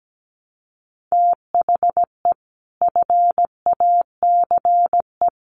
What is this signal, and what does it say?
Morse code, heard as a keyed tone, about 17 wpm, 700 Hz. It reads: THE FACE
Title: Sets of 3 Words 17wpm